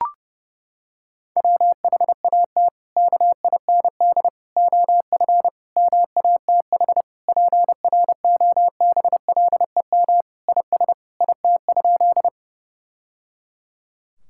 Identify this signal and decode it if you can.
Morse code, heard as a keyed tone, about 30 wpm, 700 Hz. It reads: WHAT KIND OF MATH PROBLEM IS IT?
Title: Sentences from Top 1000 Words - Review Part 6 - 30wpm